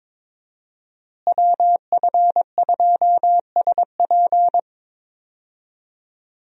Morse code, heard as a keyed tone, about 22 wpm, 700 Hz. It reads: WF2SP
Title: USA Call Signs - 5 Characters - Repeated 3x - 22wpm